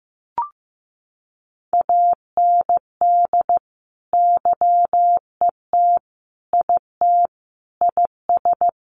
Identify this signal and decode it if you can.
Morse code, heard as a keyed tone, about 15 wpm, 700 Hz. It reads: AND YET IT IS
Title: Sentences from Top 1500 Words 15wpm